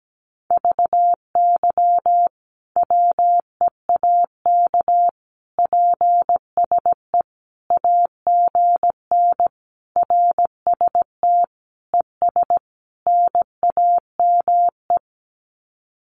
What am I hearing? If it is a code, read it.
Morse code, heard as a keyed tone, about 17 wpm, 700 Hz. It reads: VY WEAK PSE AGN RST ES NAME
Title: Common QSO Phrases 17wpm